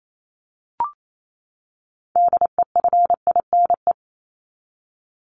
Morse code, heard as a keyed tone, about 28 wpm, 700 Hz. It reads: DEFINE
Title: ED Suffix - Top 100 Words with Base Words 28wpm